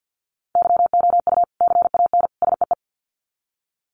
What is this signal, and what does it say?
Morse code, heard as a keyed tone, about 50 wpm, 700 Hz. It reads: YOU CAN SEE
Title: Sets of 3 Words 50wpm